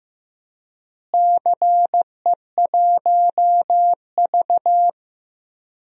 Morse code, heard as a keyed tone, about 15 wpm, 700 Hz. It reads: CE1V
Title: DXCC-100 Call Signs - 4 Characters - Repeated 2x - 15wpm